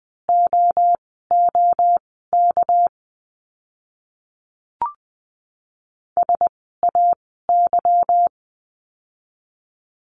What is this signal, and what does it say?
Morse code, heard as a keyed tone, about 20 wpm, spaced as for 14 wpm, 700 Hz. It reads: OOK SAY
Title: 199 Lesson 32f - Letter k - Words 20wpm